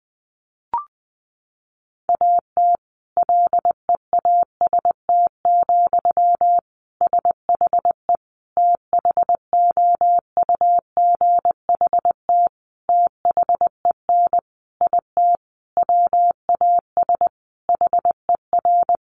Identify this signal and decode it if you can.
Morse code, heard as a keyed tone, about 20 wpm, 700 Hz. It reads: AT LEAST, SHE THOUGHT THEN IT WAS HER
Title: Sentences from Top 1400 Words 20wpm